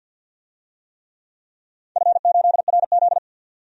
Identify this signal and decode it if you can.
Morse code, heard as a keyed tone, about 52 wpm, 700 Hz. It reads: F8RZ
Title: DXCC-100 Call Signs - 4 Characters - 50wpm